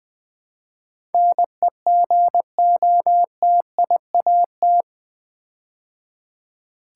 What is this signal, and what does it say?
Morse code, heard as a keyed tone, about 20 wpm, 700 Hz. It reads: NEGOTIAT
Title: Build A Word - 4 Letters At A Time - 11-Letter Words - 20wpm